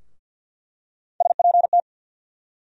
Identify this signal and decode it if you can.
Morse code, heard as a keyed tone, about 53 wpm, 700 Hz. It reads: SPT
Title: USA Call Sign Postfixes - 3 Letters 50wpm